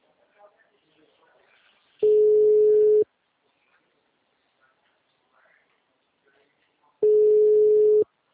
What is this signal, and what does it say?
Ringback tone (Europe, 425 Hz), with speech.